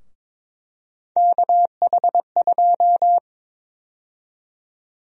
Morse code, heard as a keyed tone, about 22 wpm, 700 Hz. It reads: KH2
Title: USA Call Sign Prefixes - 22wpm